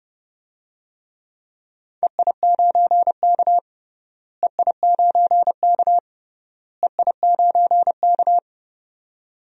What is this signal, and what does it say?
Morse code, heard as a keyed tone, about 30 wpm, 700 Hz. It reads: EI9K EI9K EI9K
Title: DXCC-100 Call Signs - 4 Characters - Repeated 3x - 30wpm